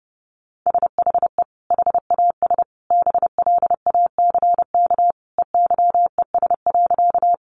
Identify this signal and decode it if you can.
Morse code, heard as a keyed tone, about 30 wpm, 700 Hz. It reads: SHE HAS BLACK EYES.
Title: Easy Phrases 30wpm